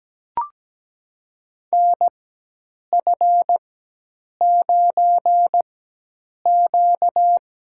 Morse code, heard as a keyed tone, about 17 wpm, spaced as for 8 wpm, 700 Hz. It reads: NF9Q
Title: USA Call Signs - 4 Characters - 4x Character Spacing - 17wpm